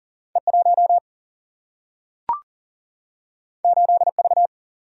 Morse code, heard as a keyed tone, about 41 wpm, 700 Hz. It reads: E1 8V
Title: Single-Letter Single-Number 40wpm